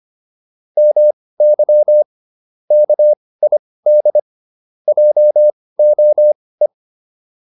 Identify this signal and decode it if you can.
Morse code, heard as a keyed tone, about 25 wpm, spaced as for 18 wpm, 600 Hz. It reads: MY KID JOE